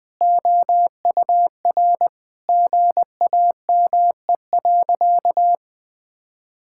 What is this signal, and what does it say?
Morse code, heard as a keyed tone, about 20 wpm, 700 Hz. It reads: OUR GAME.
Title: Sentences from Top 1000 Words - Review Part 5 - 20wpm